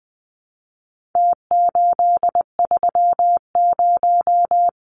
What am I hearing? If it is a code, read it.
Morse code, heard as a keyed tone, about 20 wpm, 700 Hz. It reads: T830
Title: Numbers - Contest Style - 4-Digit 20wpm